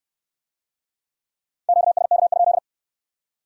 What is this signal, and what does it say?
Morse code, heard as a keyed tone, about 64 wpm, 700 Hz. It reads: QSK?